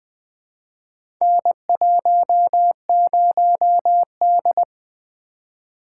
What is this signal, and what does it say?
Morse code, heard as a keyed tone, about 20 wpm, 700 Hz. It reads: N10D
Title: Four Letter-Number Combinations - 20wpm